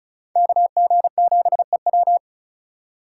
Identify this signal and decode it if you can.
Morse code, heard as a keyed tone, about 47 wpm, spaced as for 41 wpm, 700 Hz. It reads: KG7EW
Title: Speed Racing - USA Call Signs 35wpm